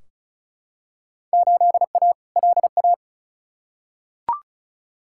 Morse code, heard as a keyed tone, about 36 wpm, 700 Hz. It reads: 8A LA
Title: Field Day Exchanges Encore - 35wpm